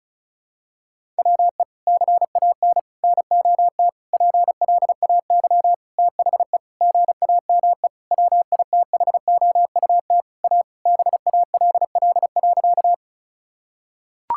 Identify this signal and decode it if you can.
Morse code, heard as a keyed tone, about 36 wpm, 700 Hz. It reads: WE CAN NOT PLAY THE GAME WITHOUT A BALL.
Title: Sentences from Top 1000 Words - Review Part 6 - 35wpm